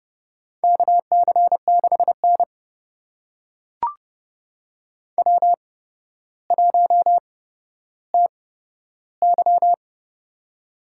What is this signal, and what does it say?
Morse code, heard as a keyed tone, about 31 wpm, spaced as for 13 wpm, 700 Hz. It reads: KC6N W1TY
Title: USA Call Signs - 4 Characters - 8x Character Spacing - 30wpm